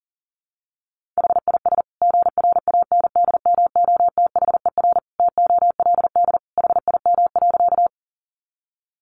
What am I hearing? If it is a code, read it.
Morse code, heard as a keyed tone, about 40 wpm, 700 Hz. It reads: HIS GRANDMOTHER TOLD HIM.
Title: Sentences from Top 1200 Words 40wpm